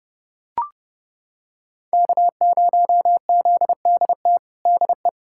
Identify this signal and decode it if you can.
Morse code, heard as a keyed tone, about 32 wpm, 700 Hz. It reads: K0ZDT DE
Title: USA Callsign DE Callsign 30wpm